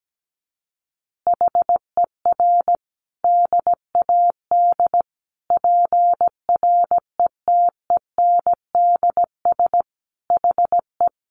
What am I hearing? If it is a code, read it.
Morse code, heard as a keyed tone, about 17 wpm, 700 Hz. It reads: HER DAD PRETENDS HE
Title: Additional Easy Phrases 17wpm